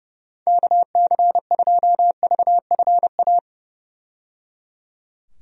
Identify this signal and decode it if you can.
Morse code, heard as a keyed tone, about 30 wpm, 700 Hz. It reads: KC2VFA
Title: USA Call Signs - 6 Characters - 30wpm